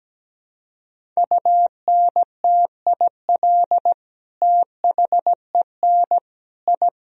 Morse code, heard as a keyed tone, about 17 wpm, 700 Hz. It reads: UNTIL THEN I
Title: Sentences from Top 1000 Words - Review Part 1 - 17wpm